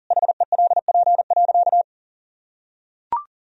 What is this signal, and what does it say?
Morse code, heard as a keyed tone, about 40 wpm, 700 Hz. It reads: HELP.
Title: Sentences from Top 1000 Words - Review Part 4 - 40wpm